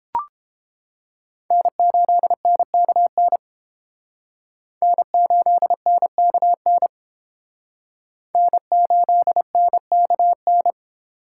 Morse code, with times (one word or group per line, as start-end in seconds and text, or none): 1.5-3.4 N8NKN
4.8-6.9 N8NKN
8.3-10.7 N8NKN